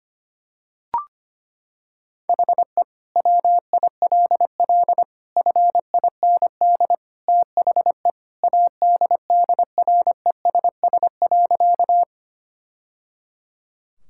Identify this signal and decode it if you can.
Morse code, heard as a keyed tone, about 25 wpm, 700 Hz. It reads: HE WILL FIND THE ADDRESS.
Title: Sentences from Top 1000 Words - Review Part 6 - 25wpm